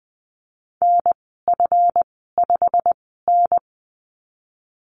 Morse code, heard as a keyed tone, about 20 wpm, spaced as for 15 wpm, 700 Hz. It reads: NF5N
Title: 193 Lesson 31g - Letter v - Call Signs 20wpm